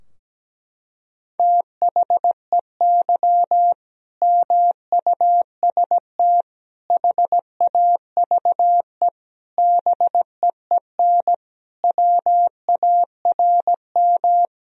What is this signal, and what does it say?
Morse code, heard as a keyed tone, about 17 wpm, 700 Hz. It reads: THEY MUST HAVE BEEN WARM
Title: Sentences from Top 1000 Words - Review Part 2 - 17wpm